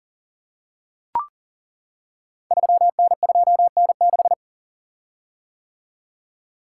Morse code, heard as a keyed tone, about 39 wpm, 690 Hz. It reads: 3N2N6
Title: Code Groups - 5 Letter:Numbers - 40wpm